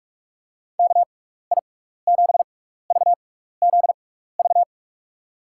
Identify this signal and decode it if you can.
Morse code, heard as a keyed tone, about 44 wpm, spaced as for 15 wpm, 700 Hz. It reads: KI7VZV
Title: USA Call Signs - 6 Characters - 6x Character Spacing - 45wpm